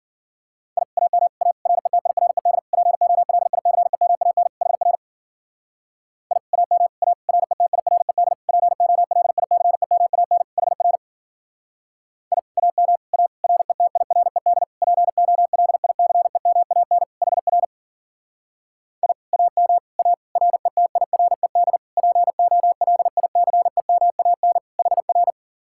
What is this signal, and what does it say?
Morse code, heard as a keyed tone, about 52 wpm, 700 Hz. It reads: I AM A RETIRED POLICEMAN HR I AM A RETIRED POLICEMAN HR I AM A RETIRED POLICEMAN HR I AM A RETIRED POLICEMAN HR